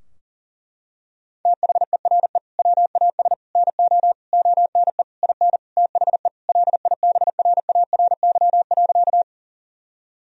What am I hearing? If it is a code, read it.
Morse code, heard as a keyed tone, about 40 wpm, 700 Hz. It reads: THERE WAS NO ONE IN THE LIBRARY.
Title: Sentences from Top 1000 Words - Review Part 4 - 40wpm